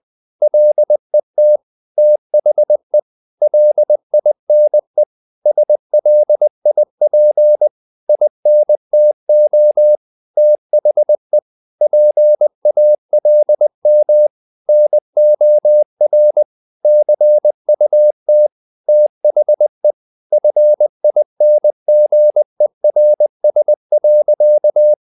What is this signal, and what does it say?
Morse code, heard as a keyed tone, about 20 wpm, 600 Hz. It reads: LET THE LINE SLIP INTO THE PALM NOR CUT THE FINGERS.
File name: the old man and the sea part 2 20wpm